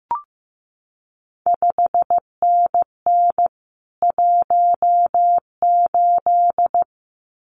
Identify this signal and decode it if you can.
Morse code, heard as a keyed tone, about 15 wpm, 700 Hz. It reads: 5NN 18
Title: CQ WW DX Contest Exchange Encore - 15wpm